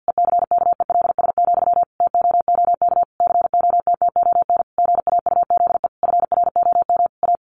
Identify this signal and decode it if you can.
Morse code, heard as a keyed tone, about 49 wpm, 700 Hz. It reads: ECKELS, TOOK COTTON GAUZE FROM A
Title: A Sound Of Thunder 50wpm